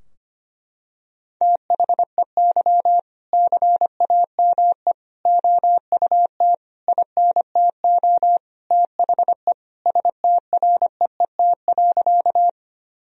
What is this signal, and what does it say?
Morse code, heard as a keyed tone, about 25 wpm, 700 Hz. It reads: THEY CAME OUT INTO THE STREET.
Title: Sentences from Top 1000 Words - Review Part 3 - 25wpm